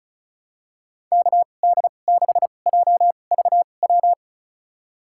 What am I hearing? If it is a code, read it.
Morse code, heard as a keyed tone, about 36 wpm, spaced as for 25 wpm, 700 Hz. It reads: KD6JVW
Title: USA Call Signs - 6 Characters - 2x Character Spacing - 35wpm